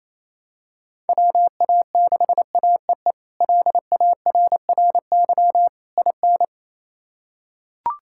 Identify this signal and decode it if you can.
Morse code, heard as a keyed tone, about 28 wpm, 700 Hz. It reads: WA6AEE LARRY IN
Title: K1USN SST QSO Practice - 28wpm